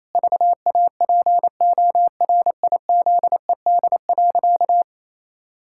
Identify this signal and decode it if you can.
Morse code, heard as a keyed tone, about 28 wpm, 700 Hz. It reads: VAPORIZED.
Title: Pangram Challenge v2 - 28wpm